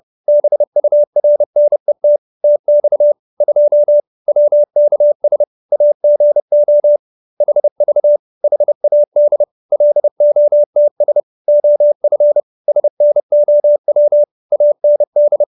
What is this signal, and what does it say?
Morse code, heard as a keyed tone, about 30 wpm, 590 Hz. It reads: BURNET TX 2 WKS AGO HV HAD LOTS OF SNOW AND